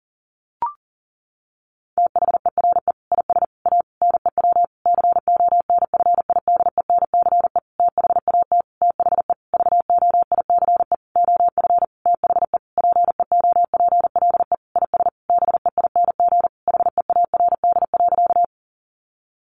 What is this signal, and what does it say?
Morse code, heard as a keyed tone, about 40 wpm, 700 Hz. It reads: THERE IS A NEW CONFIDENCE THAT THE VOICE OF THE PEOPLE IS BEING HEARD.
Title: Sentences from Top 1500 Words 40wpm